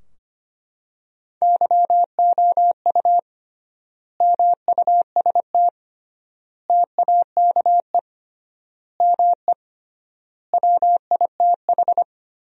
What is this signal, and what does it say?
Morse code, heard as a keyed tone, about 25 wpm, 700 Hz. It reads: YOU MUST TAKE ME WITH
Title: 3x Word-Spacing - Sentences from Top 200 Words - 25wpm